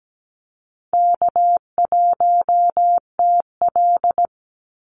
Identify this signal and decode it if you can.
Morse code, heard as a keyed tone, about 17 wpm, 700 Hz. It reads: K1TL
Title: Easy Call Signs Repeated 2x - 17wpm